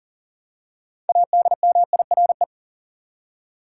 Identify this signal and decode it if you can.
Morse code, heard as a keyed tone, about 40 wpm, 700 Hz. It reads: ADMIRE
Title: Six Letter Words - 40wpm